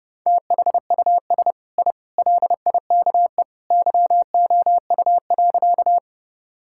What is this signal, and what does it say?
Morse code, heard as a keyed tone, about 30 wpm, 700 Hz. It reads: THUS I LIKE YOU.
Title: Sentences from Top 1000 Words - Review Part 5 - 30wpm